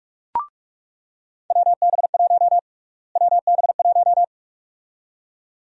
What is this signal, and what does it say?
Morse code, heard as a keyed tone, about 42 wpm, 700 Hz. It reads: WB1 WB1
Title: USA Call Sign Prefixes - Repeated 2x - 45wpm